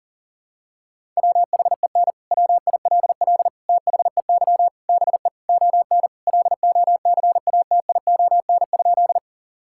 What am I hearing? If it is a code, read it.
Morse code, heard as a keyed tone, about 37 wpm, 700 Hz. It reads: WHEN WILL THEY BE ON LOCATION?
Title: Sentences from Top 1000 Words - Review Part 6 - 40wpm